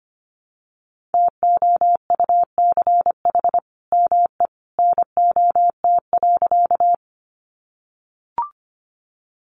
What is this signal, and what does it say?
Morse code, heard as a keyed tone, about 25 wpm, 700 Hz. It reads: TOUCH ME NOT.
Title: Sentences from Top 1200 Words 25wpm